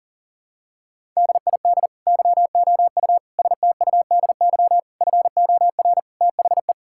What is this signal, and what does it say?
Morse code, heard as a keyed tone, about 40 wpm, 700 Hz. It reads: DID YOU STUDY FOR THE
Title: Sentences from Top 1000 Words - Review Part 5 - 40wpm